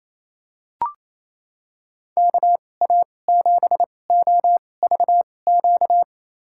Morse code, heard as a keyed tone, about 28 wpm, spaced as for 20 wpm, 700 Hz. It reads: KA7OVQ
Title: USA Call Signs - 6 Characters - 2x Character Spacing - 28wpm